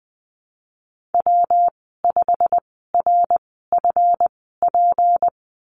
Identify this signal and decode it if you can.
Morse code, heard as a keyed tone, about 20 wpm, spaced as for 15 wpm, 700 Hz. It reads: W5RFP
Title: 161 Lesson 26g - Letter g - Call Signs 20wpm